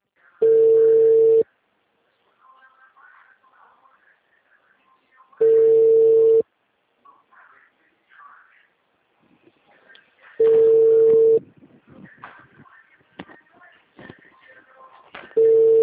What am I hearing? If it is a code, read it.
Ringback tone (China, 450 Hz), with speech.